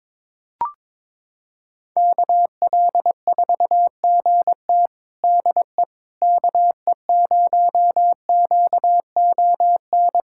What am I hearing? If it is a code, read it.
Morse code, heard as a keyed tone, about 22 wpm, 700 Hz. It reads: KL4GT DE KE0QON
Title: USA Callsign DE Callsign 22wpm